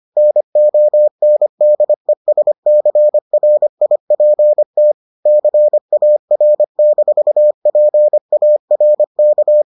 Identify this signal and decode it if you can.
Morse code, heard as a keyed tone, about 25 wpm, 600 Hz. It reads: NONDESCRIPT CAR-PARK